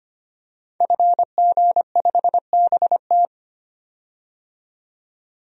Morse code, heard as a keyed tone, about 25 wpm, 700 Hz. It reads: FG5BT
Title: DXCC-100 Call Signs - 5 Characters - 25wpm